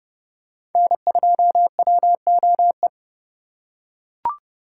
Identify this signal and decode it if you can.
Morse code, heard as a keyed tone, about 29 wpm, 700 Hz. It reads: N2WOE